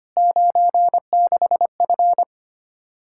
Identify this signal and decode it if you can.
Morse code, heard as a keyed tone, about 25 wpm, 700 Hz. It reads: 96F
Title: Practice QSO - Exchange 2 - Info - 25wpm